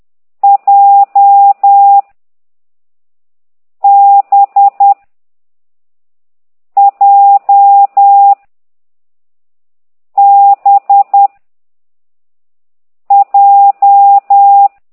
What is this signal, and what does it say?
Morse code, heard as a keyed tone, about 10 wpm, 800 Hz. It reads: J B J B J